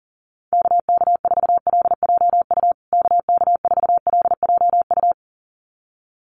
Morse code, heard as a keyed tone, about 40 wpm, 700 Hz. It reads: KK4LJU KK4LJU
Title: USA Call Signs Repeated 2x 40wpm